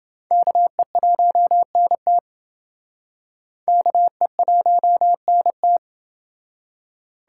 Morse code, heard as a keyed tone, about 29 wpm, 700 Hz. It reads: KE1NT KE1NT